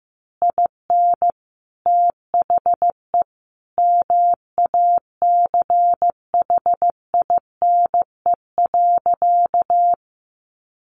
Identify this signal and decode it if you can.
Morse code, heard as a keyed tone, about 15 wpm, 700 Hz. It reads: IN THE MACHINE.
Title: A Sound Of Thunder 15wpm